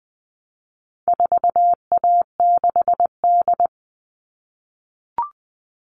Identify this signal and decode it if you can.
Morse code, heard as a keyed tone, about 20 wpm, 700 Hz. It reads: 4A6D
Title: All Cut Numbers - 4-Digit 20wpm